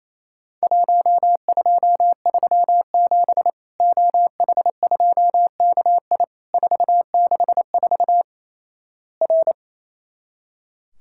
Morse code, heard as a keyed tone, about 27 wpm, 700 Hz. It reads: 1237 OH2KI 464 R
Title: Worked All Europe DX Contest - QTC Exchange - 28wpm